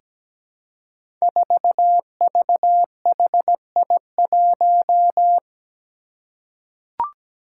Morse code, heard as a keyed tone, about 17 wpm, 700 Hz. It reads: 4VHI1
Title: Five Challenging Letter-Number Combinations 17wpm